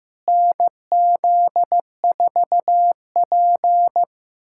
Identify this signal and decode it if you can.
Morse code, heard as a keyed tone, about 15 wpm, 700 Hz. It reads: NZ4P
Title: USA Callsign DE Callsign - 4 Characters - 15wpm